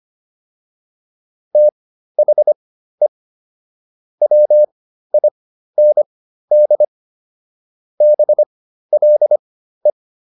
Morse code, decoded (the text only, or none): THE WIND BLE